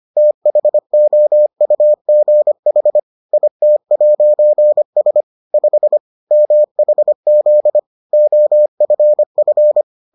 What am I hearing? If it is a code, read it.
Morse code, heard as a keyed tone, about 25 wpm, 600 Hz. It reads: THOUGH IT'S 5 MHZ OFF